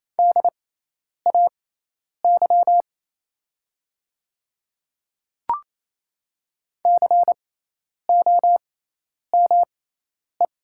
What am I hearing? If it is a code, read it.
Morse code, heard as a keyed tone, about 28 wpm, spaced as for 10 wpm, 700 Hz. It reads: DAY COME
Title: 6x Character Spacing - Top 100 Words - 28wpm